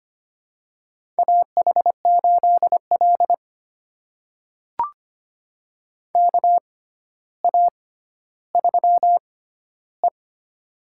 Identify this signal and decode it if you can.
Morse code, heard as a keyed tone, about 25 wpm, spaced as for 13 wpm, 700 Hz. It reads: AH8L KA3E
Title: USA Call Signs - 4 Characters - 6x Character Spacing - 25wpm